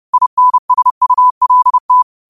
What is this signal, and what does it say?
Morse code, heard as a keyed tone, about 30 wpm, 1000 Hz. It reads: INSULT
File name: qotdfiles / 30wpm